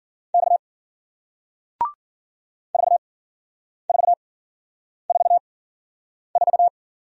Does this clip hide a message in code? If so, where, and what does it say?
Yes: Morse code from 0.3 to 6.7 s, heard as a keyed tone, about 53 wpm, spaced as for 9 wpm, 700 Hz. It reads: X 4444